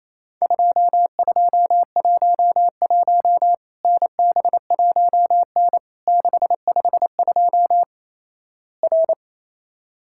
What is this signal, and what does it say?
Morse code, heard as a keyed tone, about 28 wpm, 700 Hz. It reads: 2211 NB1N 652 R
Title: Worked All Europe DX Contest - QTC Exchange - 28wpm